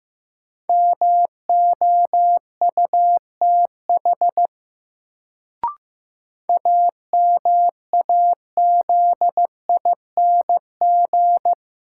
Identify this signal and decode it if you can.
Morse code, heard as a keyed tone, about 15 wpm, 700 Hz. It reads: MOUTH AMAZING
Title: Sentences from Top 1500 Words - New Word Drill Alt - 15wpm